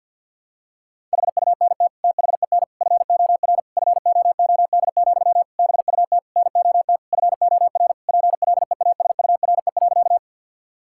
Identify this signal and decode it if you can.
Morse code, heard as a keyed tone, about 50 wpm, 700 Hz. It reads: HUNT THEN FOR FOOD, BUT NOT FOR PLEASURE.
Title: Sentences from Top 1600 Words 50wpm